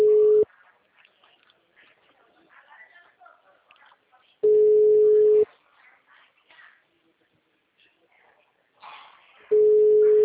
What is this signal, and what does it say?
Ringback tone (Europe, 425 Hz), with speech.
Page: Limeira - Telemarket